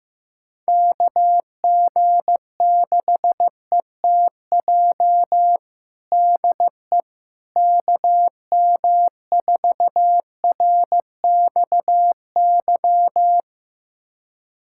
Morse code, heard as a keyed tone, about 16 wpm, 700 Hz. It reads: KG6ETJ DE KM4RXY
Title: USA Callsign DE Callsign 15wpm